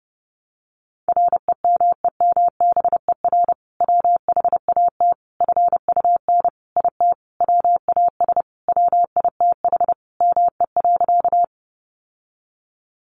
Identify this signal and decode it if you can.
Morse code, heard as a keyed tone, about 30 wpm, 700 Hz. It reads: REMEMBER WHAT FUN IT WAS WITH ME.
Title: Sentences from Top 700 Words 30wpm